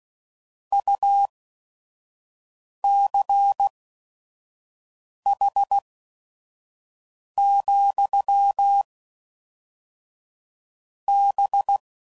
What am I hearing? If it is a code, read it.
Morse code, heard as a keyed tone, about 16 wpm, spaced as for 5 wpm, 800 Hz. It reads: UCH, B
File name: CW Code Groups @5WPM / QSOs / QSO WD0FBT